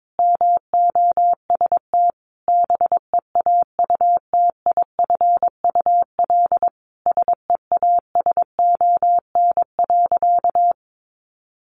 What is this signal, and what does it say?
Morse code, heard as a keyed tone, about 22 wpm, 700 Hz. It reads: MOST BEAUTIFUL SEASON.
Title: Sentences from Top 1400 Words 22wpm